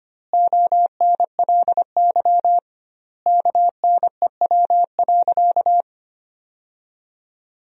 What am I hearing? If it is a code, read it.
Morse code, heard as a keyed tone, about 25 wpm, 700 Hz. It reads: ONLY KNEW.
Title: 2x Word-Spacing - Sentences from Top 500 Words - 25wpm